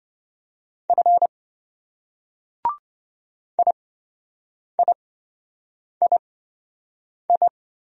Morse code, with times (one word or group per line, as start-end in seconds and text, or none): 0.9-1.3 F
3.6-7.5 IIII